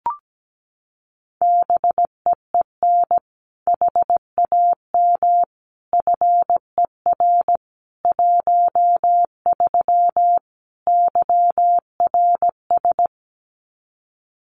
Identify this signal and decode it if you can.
Morse code, heard as a keyed tone, about 17 wpm, 700 Hz. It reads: BEEN HAM FER 13 YRS
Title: Common QSO Phrases 17wpm